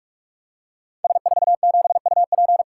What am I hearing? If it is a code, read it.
Morse code, heard as a keyed tone, about 44 wpm, 700 Hz. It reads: S47UP
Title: Code Groups - 5 Letter:Numbers - 45wpm